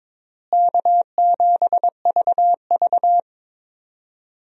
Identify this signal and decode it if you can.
Morse code, heard as a keyed tone, about 22 wpm, 700 Hz. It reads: K7VV
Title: USA Call Signs - 4 Characters - Repeated 3x - 22wpm